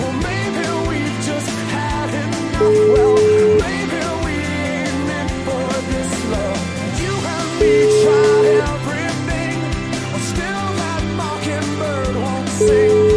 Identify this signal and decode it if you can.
Ringback tone (Europe, 425 Hz), with noise.